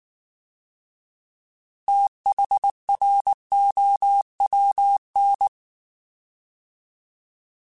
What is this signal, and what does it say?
Morse code, heard as a keyed tone, about 19 wpm, 800 Hz. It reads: THROWN
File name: Fast TH @18